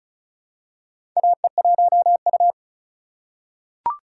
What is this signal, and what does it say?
Morse code, heard as a keyed tone, about 34 wpm, 700 Hz. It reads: AE1U
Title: USA Call Signs - 4 Characters - 6x Character Spacing - 35wpm